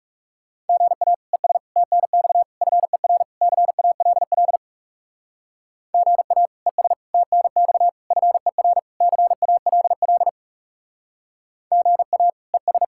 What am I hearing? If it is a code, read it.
Morse code, heard as a keyed tone, about 37 wpm, 700 Hz. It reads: GA ES TNX FER CALL GA ES TNX FER CALL GA ES
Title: Speed-Racing Common QSO Phrases - 30wpm